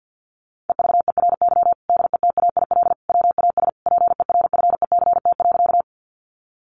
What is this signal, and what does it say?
Morse code, heard as a keyed tone, about 50 wpm, 700 Hz. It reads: EVERY DETAIL WAS PERFECT.
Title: Sentences from Top 1500 Words 50wpm